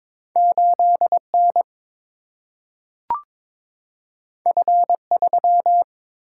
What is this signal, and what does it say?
Morse code, heard as a keyed tone, about 22 wpm, 700 Hz. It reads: 8 N F 3
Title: Single-Letter Single-Number 22wpm